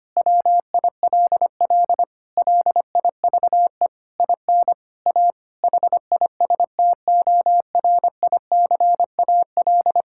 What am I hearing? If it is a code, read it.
Morse code, heard as a keyed tone, about 25 wpm, 700 Hz. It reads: WILL LIVE IN A HISTORICAL
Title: Sentences from Top 1000 Words - Review Part 5 - 25wpm